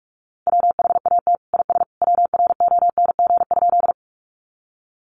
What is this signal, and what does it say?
Morse code, heard as a keyed tone, about 44 wpm, 700 Hz. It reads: WHAT IS WRONG?
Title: Sentences from Top 900 Words 45wpm